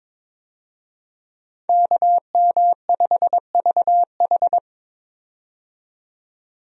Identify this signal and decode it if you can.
Morse code, heard as a keyed tone, about 22 wpm, 700 Hz. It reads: KM5VH
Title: USA Call Signs - 5 Characters - 22wpm